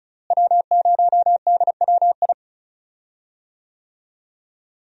Morse code, heard as a keyed tone, about 36 wpm, 700 Hz. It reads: W0DWI
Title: USA Call Signs - 5 Characters - Repeated 2x - 35wpm